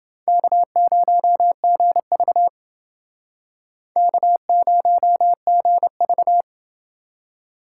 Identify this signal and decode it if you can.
Morse code, heard as a keyed tone, about 29 wpm, 700 Hz. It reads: K0GV K0GV